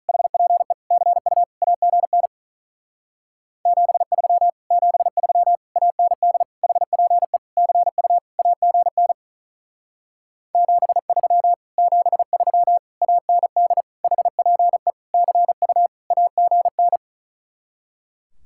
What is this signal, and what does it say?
Morse code, heard as a keyed tone, about 44 wpm, 700 Hz. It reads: HPE CU AGN 73 73 AND HPE CU AGN 73 73 AND HPE CU AGN